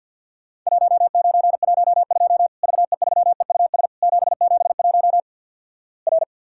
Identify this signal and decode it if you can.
Morse code, heard as a keyed tone, about 52 wpm, 700 Hz. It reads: 1912 VE3EUS 771 R
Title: Worked All Europe DX Contest - QTC Exchange - 50wpm